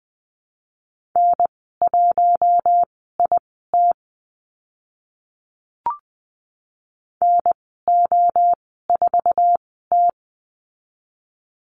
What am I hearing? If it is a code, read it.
Morse code, heard as a keyed tone, about 20 wpm, spaced as for 14 wpm, 700 Hz. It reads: N1IT NO4T
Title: 062 Lesson 11g - Letter h - Call Signs 20wpm